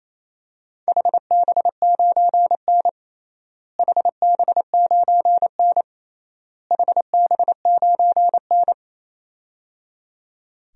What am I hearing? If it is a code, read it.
Morse code, heard as a keyed tone, about 27 wpm, 700 Hz. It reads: HB9N HB9N HB9N
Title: DXCC-100 Call Signs - 4 Characters - Repeated 3x - 28wpm